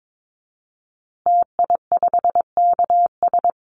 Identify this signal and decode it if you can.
Morse code, heard as a keyed tone, about 22 wpm, 700 Hz. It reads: TI5KS
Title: Call Signs Repeated 3x - 22wpm